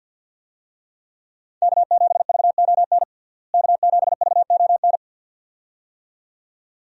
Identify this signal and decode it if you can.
Morse code, heard as a keyed tone, about 49 wpm, 700 Hz. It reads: K7VON K7VON
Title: USA Call Signs - 5 Characters - Repeated 2x - 50wpm